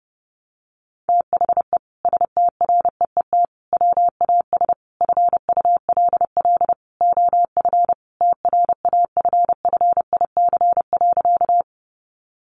Morse code, heard as a keyed tone, about 30 wpm, 700 Hz. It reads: THE STREET WAS FULL OF TRAFFIC.
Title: Sentences from Top 1300 Words 30wpm